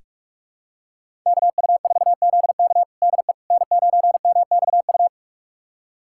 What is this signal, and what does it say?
Morse code, heard as a keyed tone, about 47 wpm, 700 Hz. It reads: KU4ZK DE N9MXU
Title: USA Callsign DE Callsign - 5 Characters - 45wpm